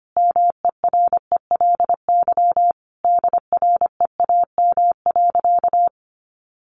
Morse code, heard as a keyed tone, about 25 wpm, 700 Hz. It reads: MERELY DREAM.
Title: Sentences from Top 1500 Words 25wpm